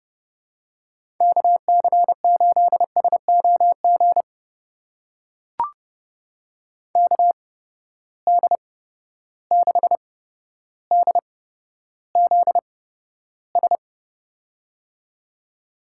Morse code, heard as a keyed tone, about 30 wpm, spaced as for 13 wpm, 700 Hz. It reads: KC8SOG KD6DZS
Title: USA Call Signs - 6 Characters - 8x Character Spacing - 30wpm